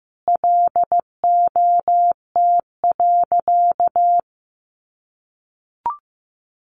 Morse code, heard as a keyed tone, about 15 wpm, 700 Hz. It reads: LOT.
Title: Sentences from Top 1500 Words 15wpm